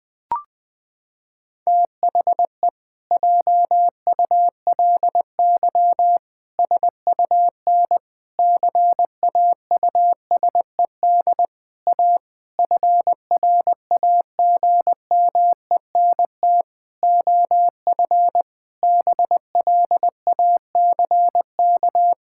Morse code, heard as a keyed tone, about 20 wpm, 700 Hz. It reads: THE JULY SUN CAUSED A FRAGMENT OF BLACK
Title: Pangram Challenge v2 - 20wpm